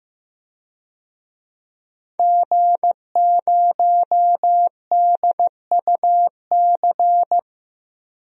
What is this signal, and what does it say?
Morse code, heard as a keyed tone, about 15 wpm, 700 Hz. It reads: G0DUC
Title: DXCC-100 Call Signs - 5 Characters - 15wpm